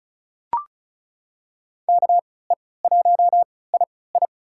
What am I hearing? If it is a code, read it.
Morse code, heard as a keyed tone, about 36 wpm, spaced as for 20 wpm, 700 Hz. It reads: KE1II
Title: USA Call Signs - 5 Characters - 3x Character Spacing - 35wpm